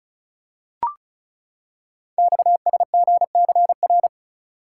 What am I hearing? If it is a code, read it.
Morse code, heard as a keyed tone, about 36 wpm, 700 Hz. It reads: XSGCR
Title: Code Groups - 5 Letters - 35wpm